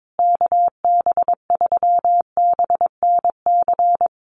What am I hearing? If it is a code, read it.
Morse code, heard as a keyed tone, about 22 wpm, 700 Hz. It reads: KB3BNC
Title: USA Call Signs Repeated 3x 22wpm